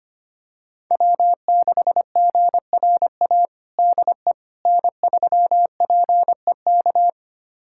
Morse code, heard as a keyed tone, about 25 wpm, 700 Hz. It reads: W6GRA DE N3PEK
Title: USA Callsign DE Callsign 25wpm